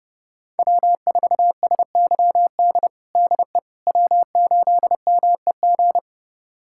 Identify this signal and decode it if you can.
Morse code, heard as a keyed tone, about 30 wpm, 700 Hz. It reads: W4SYD DE W8MEG
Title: USA Callsign DE Callsign - 5 Characters - 30wpm